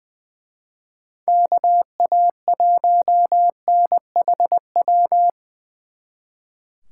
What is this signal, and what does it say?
Morse code, heard as a keyed tone, about 23 wpm, 700 Hz. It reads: KA1NHW
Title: Speed Racing - USA 2x3 Call Signs 20wpm